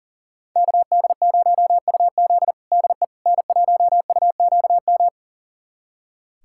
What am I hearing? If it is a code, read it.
Morse code, heard as a keyed tone, about 40 wpm, 700 Hz. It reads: KD0UZ DE N1UQM
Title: USA Callsign DE Callsign - 5 Characters - 40wpm